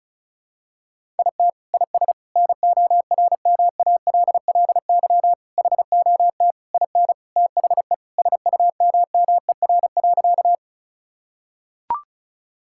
Morse code, heard as a keyed tone, about 34 wpm, 700 Hz. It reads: IT IS NORMALLY HOT IN THE SUMMER.
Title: Sentences from Top 1000 Words - Review Part 6 - 35wpm